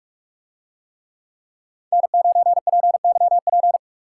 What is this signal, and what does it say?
Morse code, heard as a keyed tone, about 45 wpm, 700 Hz. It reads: N9PYP